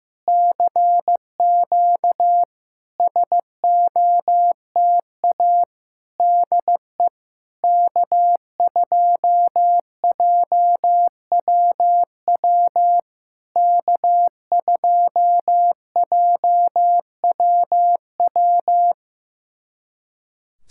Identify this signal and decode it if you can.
Morse code, heard as a keyed tone, about 15 wpm, 700 Hz. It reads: CQ SOTA DE K2JWW K2JWW
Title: SOTA QSO Practice 15wpm